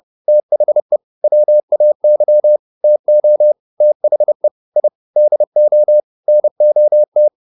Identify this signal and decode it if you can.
Morse code, heard as a keyed tone, about 30 wpm, 600 Hz. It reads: THE WAY TO THE I DO NOT